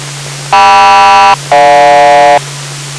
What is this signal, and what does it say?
Aircraft SELCAL, heard as two pairs of tones. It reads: KM-GJ